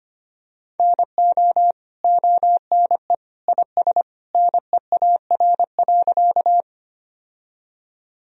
Morse code, heard as a keyed tone, about 25 wpm, 700 Hz. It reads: NO ONE IS NEAR.
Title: Sentences from Top 1000 Words - Review Part 1 - 25wpm